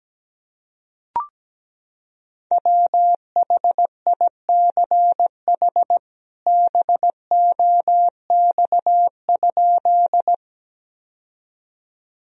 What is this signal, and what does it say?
Morse code, heard as a keyed tone, about 17 wpm, 700 Hz. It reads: WHICH BOX?
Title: Sentences from Top 1000 Words - Review Part 2 - 17wpm